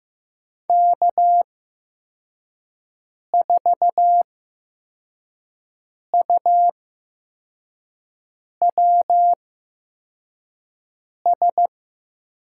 Morse code, heard as a keyed tone, about 15 wpm, spaced as for 4 wpm, 700 Hz. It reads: K4UWS
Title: USA Call Signs - 8x Character Spacing - 15wpm